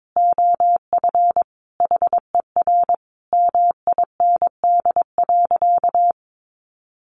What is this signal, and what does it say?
Morse code, heard as a keyed tone, about 22 wpm, 700 Hz. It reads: OF HER MIND.
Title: Sentences from Top 1300 Words 22wpm